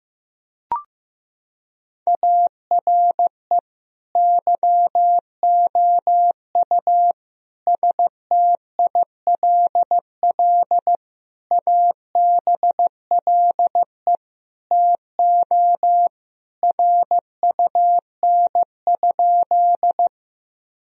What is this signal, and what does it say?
Morse code, heard as a keyed tone, about 15 wpm, 700 Hz. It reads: ARE YOU STILL ABLE TO RUN?
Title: Sentences from Top 1000 Words - Review Part 2 - 15wpm